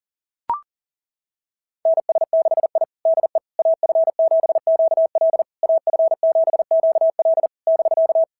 Morse code, heard as a keyed tone, about 40 wpm, 690 Hz. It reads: NS6I DE AF7QL AF7QL <BK>
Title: SOTA QSO Practice 40wpm